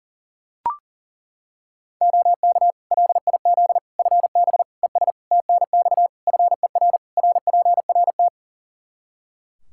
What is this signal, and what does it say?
Morse code, heard as a keyed tone, about 37 wpm, 700 Hz. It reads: OK LIZ FB ES TNX FER RPRT
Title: K1USN SST QSO Practice 40wpm